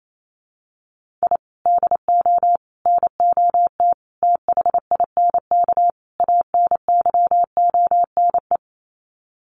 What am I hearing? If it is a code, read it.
Morse code, heard as a keyed tone, about 28 wpm, 700 Hz. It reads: I DO NOT THINK ANYONE
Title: Sets of 5 Words Encore 28wpm